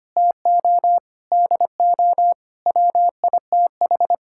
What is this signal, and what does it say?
Morse code, heard as a keyed tone, about 25 wpm, 700 Hz. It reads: TO DO WITH
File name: practice / 25wpm / Sentences with 4-Word Phrases - 25wpm